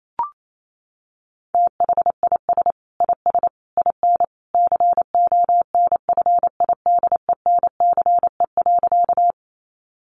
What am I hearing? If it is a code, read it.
Morse code, heard as a keyed tone, about 28 wpm, 700 Hz. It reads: THIS IS IN CONFIDENCE.
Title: Sentences from Top 1500 Words 28wpm